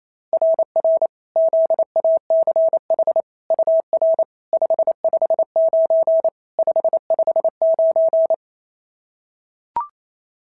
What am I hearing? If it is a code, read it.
Morse code, heard as a keyed tone, about 27 wpm, 690 Hz. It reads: RR ZACH UR 559 559
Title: SOTA QSO Practice 28wpm